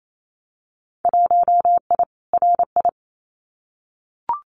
Morse code, heard as a keyed tone, about 28 wpm, 700 Hz. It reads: 1I RI
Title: Winter Field Day Exchanges Encore 28wpm